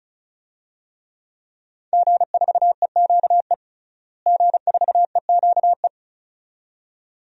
Morse code, heard as a keyed tone, about 34 wpm, 700 Hz. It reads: G4EQE G4EQE
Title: DXCC-100 Call Signs - 5 Characters - Repeated 2x - 35wpm